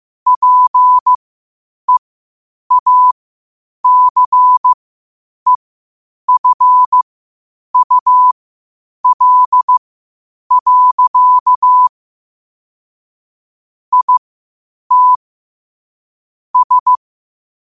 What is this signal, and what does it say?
Morse code, heard as a keyed tone, about 15 wpm, spaced as for 8 wpm, 1000 Hz. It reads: PEACEFUL. IT S
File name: qotdfiles / 8wpm